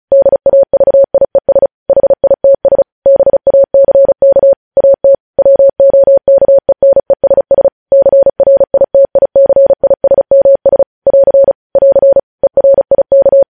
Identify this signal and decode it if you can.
Morse code, heard as a keyed tone, about 36 wpm, 550 Hz. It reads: DAVIES HITS BACK AT WOKENESS CRITICISMS <AR> <AR> ERIK
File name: podcast / News headlines-1x-35x35